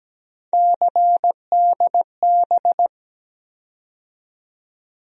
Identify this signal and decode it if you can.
Morse code, heard as a keyed tone, about 17 wpm, 700 Hz. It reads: CDB
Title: USA Call Sign Postfixes - 3 Letters 17wpm